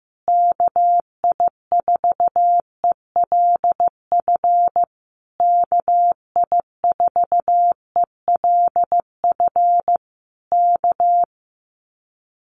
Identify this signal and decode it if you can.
Morse code, heard as a keyed tone, about 15 wpm, 700 Hz. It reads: KI4ELF KI4ELF K
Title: Common QSO Phrases Encore 15wpm